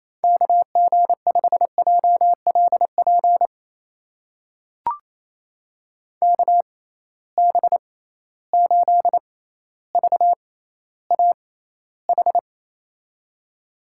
Morse code, heard as a keyed tone, about 27 wpm, spaced as for 14 wpm, 700 Hz. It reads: KG5JLP KB8VAH
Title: USA Call Signs - 6 Characters - 6x Character Spacing - 28wpm